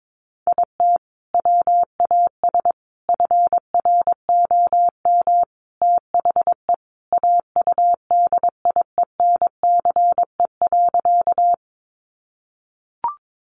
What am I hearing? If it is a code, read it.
Morse code, heard as a keyed tone, about 22 wpm, 700 Hz. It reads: IT WAS FROM THE AUDIENCE.
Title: Sentences from Top 800 Words 22wpm